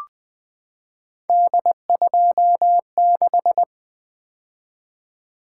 Morse code, heard as a keyed tone, about 20 wpm, 700 Hz. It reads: D26
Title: Three Letter-Number Combinations - 20wpm